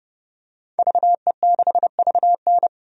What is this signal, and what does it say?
Morse code, heard as a keyed tone, about 30 wpm, 700 Hz. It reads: VE6VN
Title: DXCC-100 Call Signs - 5 Characters - 30wpm